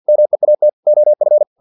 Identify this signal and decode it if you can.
Morse code, heard as a keyed tone, about 50 wpm, 600 Hz. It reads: MEAT OF
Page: Never 49wpm